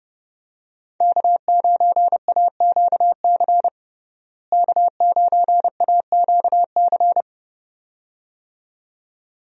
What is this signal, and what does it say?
Morse code, heard as a keyed tone, about 30 wpm, 700 Hz. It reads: K9AQC K9AQC
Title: USA Call Signs - 5 Characters - Repeated 2x - 30wpm